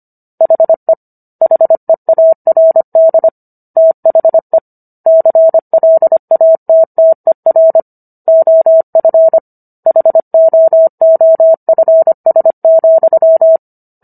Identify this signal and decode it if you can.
Morse code, heard as a keyed tone, about 25 wpm, 650 Hz. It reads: HE HEARD THE CLATTER OF HOOFS,